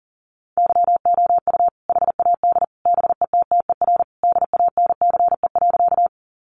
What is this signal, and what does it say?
Morse code, heard as a keyed tone, about 40 wpm, 700 Hz. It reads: YOU HAD BETTER DANCE.
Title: Sentences from Top 700 Words 40wpm